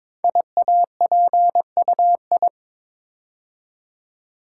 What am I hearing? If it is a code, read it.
Morse code, heard as a keyed tone, about 22 wpm, 700 Hz. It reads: IAPUI
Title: Code Groups - 5 Letters - 22wpm